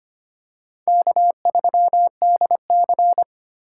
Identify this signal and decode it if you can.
Morse code, heard as a keyed tone, about 25 wpm, 700 Hz. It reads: K3DC
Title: USA Call Signs - 3x Character Spacing - 25wpm